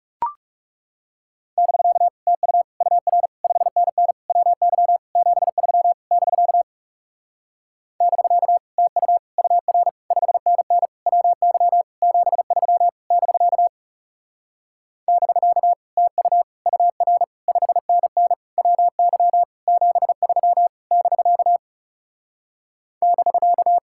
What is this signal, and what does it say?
Morse code, heard as a keyed tone, about 40 wpm, 700 Hz. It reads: <BK> TU UR 5NN WY 73 <BK> <BK> TU UR 5NN WY 73 <BK> <BK> TU UR 5NN WY 73 <BK> <BK>